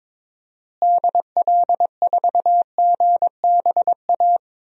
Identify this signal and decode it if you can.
Morse code, heard as a keyed tone, about 22 wpm, 700 Hz. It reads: DL4GBA
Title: DXCC-100 Call Signs - 6 Characters - Repeated 3x - 22wpm